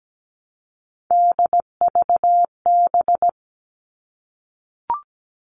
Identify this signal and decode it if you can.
Morse code, heard as a keyed tone, about 17 wpm, 700 Hz. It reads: DVB
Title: All Cut Numbers - 3-Digit 17wpm